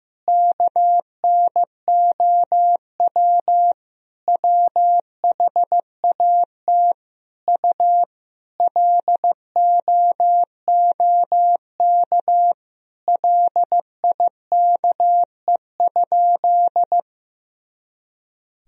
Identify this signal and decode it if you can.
Morse code, heard as a keyed tone, about 15 wpm, 700 Hz. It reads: KNOW WHAT U LOOK LIKE?
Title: Sentences from Top 100 Words and Abbreviations 15wpm